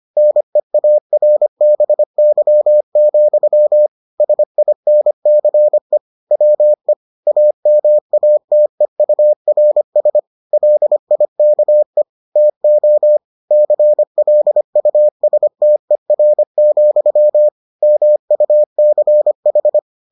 Morse code, heard as a keyed tone, about 25 wpm, 600 Hz. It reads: NEARBY, SINCE WE AMATEURS LIKE TO CLUSTER, MUCH